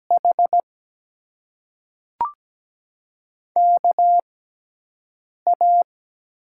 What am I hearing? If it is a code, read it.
Morse code, heard as a keyed tone, about 17 wpm, spaced as for 6 wpm, 700 Hz. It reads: H KA
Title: USA Call Signs - 6 Characters - 6x Character Spacing - 17wpm